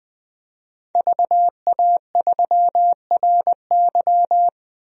Morse code, heard as a keyed tone, about 20 wpm, 700 Hz. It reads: VA3RY
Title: DXCC-100 Call Signs - 5 Characters - 20wpm